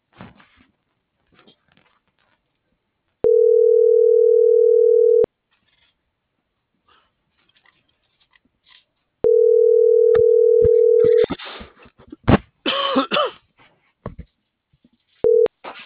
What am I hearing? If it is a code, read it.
Ringback tone (North America), 440 + 480 Hz.